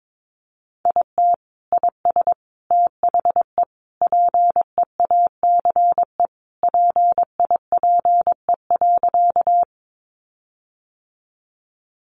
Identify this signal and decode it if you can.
Morse code, heard as a keyed tone, about 22 wpm, 700 Hz. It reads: IT IS THE PEACE PIPE.
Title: Sentences from Top 1500 Words 22wpm